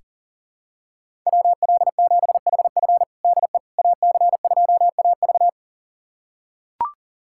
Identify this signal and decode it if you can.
Morse code, heard as a keyed tone, about 40 wpm, 700 Hz. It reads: WL7HF DE AC2AV
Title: USA Callsign DE Callsign - 5 Characters - 40wpm